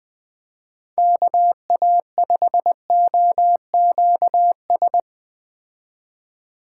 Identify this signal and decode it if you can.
Morse code, heard as a keyed tone, about 20 wpm, 700 Hz. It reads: KA5OQS
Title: Speed Racing - USA 2x3 Call Signs 17wpm